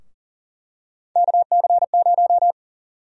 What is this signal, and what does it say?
Morse code, heard as a keyed tone, about 40 wpm, 700 Hz. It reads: KC0
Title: USA Call Sign Prefixes - 40wpm